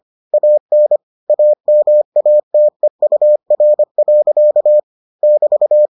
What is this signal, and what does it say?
Morse code, heard as a keyed tone, about 25 wpm, 600 Hz. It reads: AN AMATEUR. <BT>